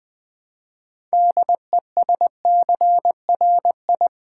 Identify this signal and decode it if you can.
Morse code, heard as a keyed tone, about 20 wpm, 700 Hz. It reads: DESCRI
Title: Build A Word - 2 Letters At A Time - 8-Letter Words - 20wpm